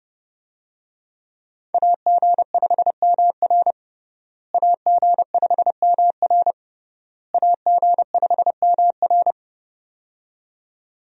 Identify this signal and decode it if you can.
Morse code, heard as a keyed tone, about 32 wpm, 700 Hz. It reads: AG5MR AG5MR AG5MR
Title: USA Call Signs - 5 Characters - Repeated 3x - 30wpm